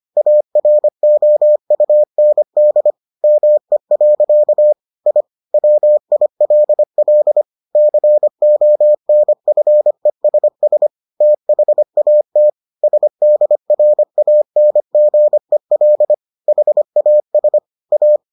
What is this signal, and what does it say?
Morse code, heard as a keyed tone, about 25 wpm, 600 Hz. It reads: AROUND ME. I WILL CONFESS THAT SDRANGEL HAS A